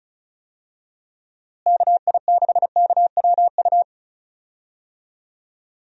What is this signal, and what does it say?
Morse code, heard as a keyed tone, about 36 wpm, 700 Hz. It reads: KI6KWU
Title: USA Call Signs - 6 Characters - Repeated 2x - 35wpm